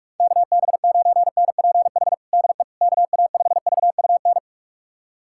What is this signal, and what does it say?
Morse code, heard as a keyed tone, about 41 wpm, 700 Hz. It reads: KB9NPH DE KA5VUN
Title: USA Callsign DE Callsign 45wpm